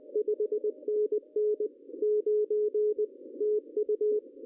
Morse code, heard as a keyed tone, about 22 wpm, 410 Hz. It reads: 5NN 9 TU
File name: CW / cqww-cw-k5tr